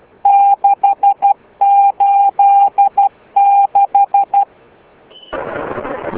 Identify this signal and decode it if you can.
Morse code, heard as a keyed tone, about 12 wpm, 800 Hz. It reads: 686